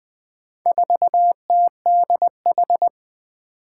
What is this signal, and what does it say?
Morse code, heard as a keyed tone, about 21 wpm, 700 Hz. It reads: 4TDH
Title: Four Challenging Letter-Number Combinations 20wpm